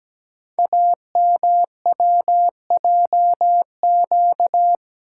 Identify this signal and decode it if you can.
Morse code, heard as a keyed tone, about 17 wpm, 700 Hz. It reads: AMWJQ
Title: Code Groups - 5 Letters - 17wpm